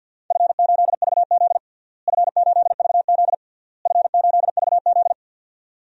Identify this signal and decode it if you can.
Morse code, heard as a keyed tone, about 51 wpm, 700 Hz. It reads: F8VZ F8VZ F8VZ
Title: DXCC-100 Call Signs - 4 Characters - Repeated 3x - 50wpm